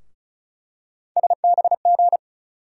Morse code, heard as a keyed tone, about 36 wpm, 700 Hz. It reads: SBG
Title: USA Call Sign Postfixes - 3 Letters 35wpm